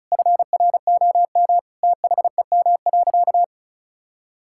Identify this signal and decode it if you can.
Morse code, heard as a keyed tone, about 36 wpm, 700 Hz. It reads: FROM THEM.
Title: Sentences from Top 1600 Words 35wpm